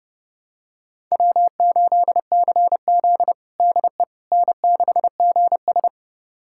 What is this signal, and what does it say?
Morse code, heard as a keyed tone, about 32 wpm, 700 Hz. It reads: W8CZ DE N6GS
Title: USA Callsign DE Callsign - 4 Characters - 30wpm